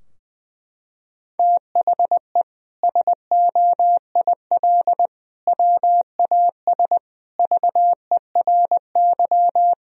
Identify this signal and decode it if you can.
Morse code, heard as a keyed tone, about 20 wpm, 700 Hz. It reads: THE SOIL WAS VERY
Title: Sentences from Top 1000 Words - Review Part 4 - 20wpm